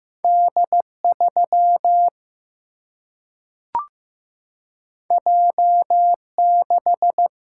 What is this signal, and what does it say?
Morse code, heard as a keyed tone, about 15 wpm, 700 Hz. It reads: D 3 J 6
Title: Single-Letter Single-Number 15wpm